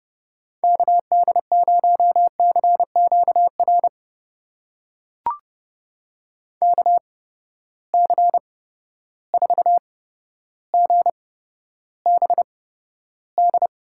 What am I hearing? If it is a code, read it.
Morse code, heard as a keyed tone, about 31 wpm, spaced as for 13 wpm, 700 Hz. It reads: KD0CQR KC4GBD
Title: USA Call Signs - 6 Characters - 8x Character Spacing - 30wpm